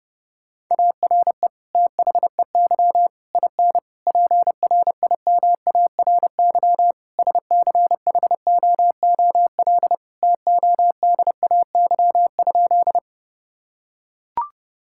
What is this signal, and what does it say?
Morse code, heard as a keyed tone, about 28 wpm, 700 Hz. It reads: ARE THEY IN PRIMARY SCHOOL TODAY?
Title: Sentences from Top 1000 Words - Review Part 5 - 30wpm